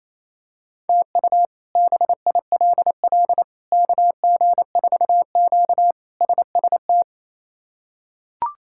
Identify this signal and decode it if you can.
Morse code, heard as a keyed tone, about 27 wpm, 700 Hz. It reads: TU BILL KG4Q SST
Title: K1USN SST QSO Practice - 28wpm